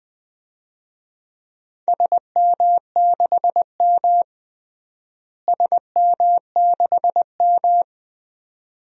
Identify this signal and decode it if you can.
Morse code, heard as a keyed tone, about 20 wpm, 700 Hz. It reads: SM6M SM6M
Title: DXCC-100 Call Signs - 4 Characters - Repeated 2x - 20wpm